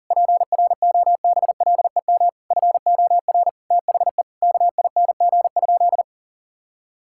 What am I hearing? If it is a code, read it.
Morse code, heard as a keyed tone, about 40 wpm, 700 Hz. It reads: PROBLEM FOR THE KING?
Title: Sentences from Top 1000 Words - Review Part 5 - 40wpm